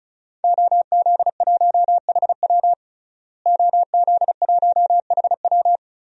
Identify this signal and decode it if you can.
Morse code, heard as a keyed tone, about 36 wpm, 700 Hz. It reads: OZ1HW OZ1HW
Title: DXCC-100 Call Signs - 5 Characters - Repeated 3x - 35wpm